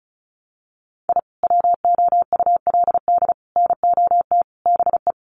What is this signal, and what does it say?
Morse code, heard as a keyed tone, about 34 wpm, 700 Hz. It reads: I WOULD NOT BE
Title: Sets of 5 Words Encore2 35wpm